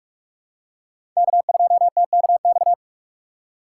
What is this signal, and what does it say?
Morse code, heard as a keyed tone, about 47 wpm, 700 Hz. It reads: K2TKX
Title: USA Call Signs - 4x Character Spacing - 45wpm